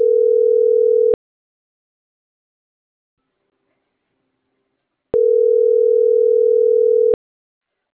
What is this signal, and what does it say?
Ringback tone (North America), 440 + 480 Hz.